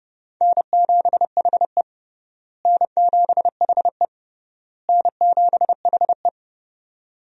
Morse code, heard as a keyed tone, about 30 wpm, 700 Hz. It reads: N7HE N7HE N7HE
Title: USA Call Signs - 4 Characters - Repeated 3x - 30wpm